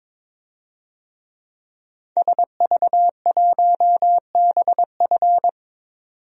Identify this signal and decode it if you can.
Morse code, heard as a keyed tone, about 22 wpm, 700 Hz. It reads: SV1BF
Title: DXCC-100 Call Signs - 5 Characters - 22wpm